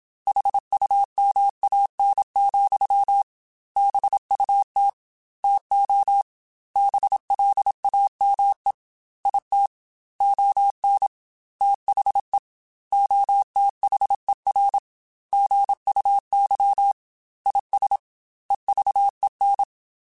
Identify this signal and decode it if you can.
Morse code, heard as a keyed tone, about 27 wpm, 800 Hz. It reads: HUMAN, BUT TO BLAME IT ON THE OTHER GUY IS EVEN
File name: Chuckles / Chuckles-mp3s / HumanErr @25